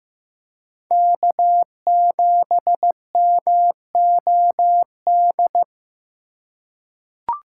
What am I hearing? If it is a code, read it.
Morse code, heard as a keyed tone, about 15 wpm, 700 Hz. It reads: K7MOD
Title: USA Call Signs - 3x Character Spacing - 15wpm